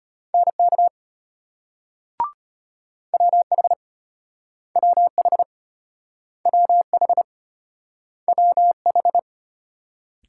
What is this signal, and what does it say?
Morse code, heard as a keyed tone, about 33 wpm, 700 Hz. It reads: NK WH WH WH WH